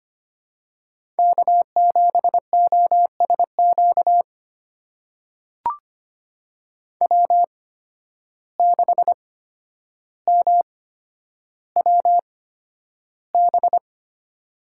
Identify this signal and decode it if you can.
Morse code, heard as a keyed tone, about 25 wpm, spaced as for 11 wpm, 700 Hz. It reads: K7OSQ W6MWB
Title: USA Call Signs - 5 Characters - 8x Character Spacing - 25wpm